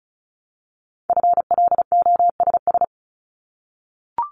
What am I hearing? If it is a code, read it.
Morse code, heard as a keyed tone, about 36 wpm, 700 Hz. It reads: FLOSS
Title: Multi-letter Phonemes 35wpm